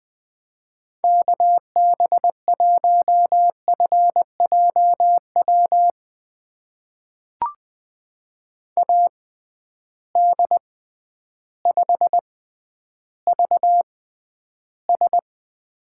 Morse code, heard as a keyed tone, about 21 wpm, spaced as for 11 wpm, 700 Hz. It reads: KB1FJW AD5VS
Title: USA Call Signs - 6x Character Spacing - 20wpm